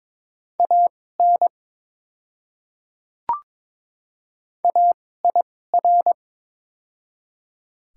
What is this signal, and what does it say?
Morse code, heard as a keyed tone, about 21 wpm, spaced as for 16 wpm, 700 Hz. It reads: AN AIR
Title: 2x Character Spacing - Top 200 Words - 22wpm